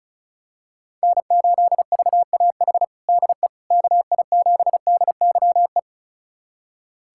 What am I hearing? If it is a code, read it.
Morse code, heard as a keyed tone, about 34 wpm, 700 Hz. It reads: N8VAH DE KI7DYE
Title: USA Callsign DE Callsign 35wpm